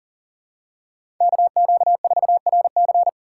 Morse code, heard as a keyed tone, about 40 wpm, 700 Hz. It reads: KQ4RC
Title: USA Call Signs - 5 Characters - 3x Character Spacing - 40wpm